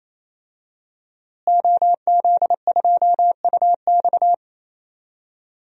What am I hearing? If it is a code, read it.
Morse code, heard as a keyed tone, about 28 wpm, 700 Hz. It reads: OZ2UX
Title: DXCC-100 Call Signs - 5 Characters - Repeated 3x - 28wpm